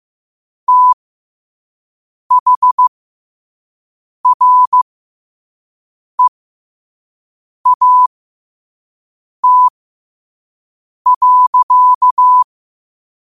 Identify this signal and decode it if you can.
Morse code, heard as a keyed tone, about 15 wpm, spaced as for 5 wpm, 1000 Hz. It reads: THREAT.
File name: qotdfiles / 5wpm